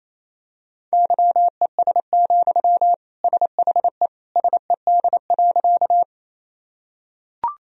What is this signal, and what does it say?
Morse code, heard as a keyed tone, about 28 wpm, 700 Hz. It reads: YES, SHE SED.
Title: Sentences from Top 500 Words and Abbreviations 28wpm